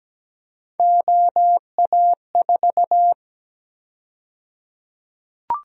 Morse code, heard as a keyed tone, about 17 wpm, 700 Hz. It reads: OA4
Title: USA Call Sign Prefixes - Repeated 2x - 17wpm